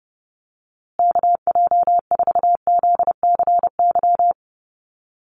Morse code, heard as a keyed tone, about 30 wpm, 700 Hz. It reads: KJ4ZCY
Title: USA Call Signs Repeated 2x 30wpm